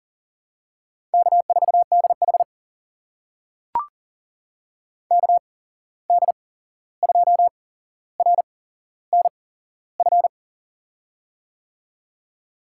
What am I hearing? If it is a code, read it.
Morse code, heard as a keyed tone, about 40 wpm, spaced as for 15 wpm, 700 Hz. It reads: K4DH KD2RNF
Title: USA Call Signs - 8x Character Spacing - 40wpm